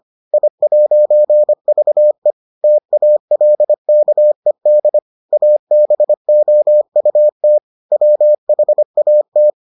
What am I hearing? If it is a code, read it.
Morse code, heard as a keyed tone, about 25 wpm, 600 Hz. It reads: I'VE TALKED ABOUT WHAT